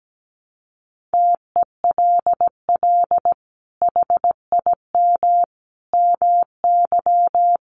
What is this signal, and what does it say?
Morse code, heard as a keyed tone, about 17 wpm, 700 Hz. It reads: TELL HIM MY
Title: Sentences from Top 1200 Words 17wpm